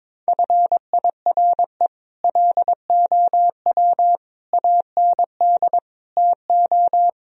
Morse code, heard as a keyed tone, about 22 wpm, 700 Hz. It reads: FIRE LOW AND TO
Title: Sentences from Top 1000 Words - Review Part 1 - 22wpm